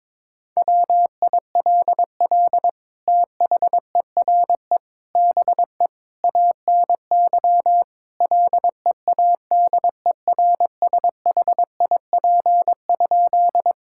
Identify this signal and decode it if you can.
Morse code, heard as a keyed tone, about 22 wpm, 700 Hz. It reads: WILL THERE BE ANY LEADERSHIP?
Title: Sentences from Top 1600 Words 22wpm